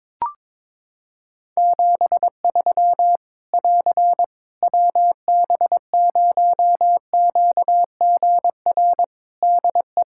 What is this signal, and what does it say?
Morse code, heard as a keyed tone, about 22 wpm, 700 Hz. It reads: 73 <AR> WB0QGR DE
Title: Practice QSO - Exchange 4 - Ending - 22wpm